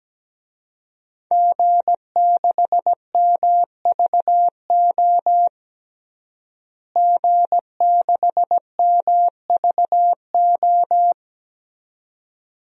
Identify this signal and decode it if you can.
Morse code, heard as a keyed tone, about 17 wpm, 700 Hz. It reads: G6MVO G6MVO
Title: DXCC-100 Call Signs - 5 Characters - Repeated 2x - 17wpm